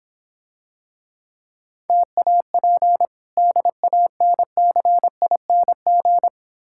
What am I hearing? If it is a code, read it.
Morse code, heard as a keyed tone, about 25 wpm, 700 Hz. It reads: TAP DANCING
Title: Difficult Words - Speed-Racing with Hints - 20wpm